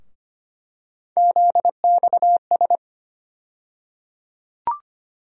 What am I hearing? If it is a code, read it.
Morse code, heard as a keyed tone, about 25 wpm, 700 Hz. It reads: ZXS
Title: USA Call Sign Postfixes - 3 Letters 25wpm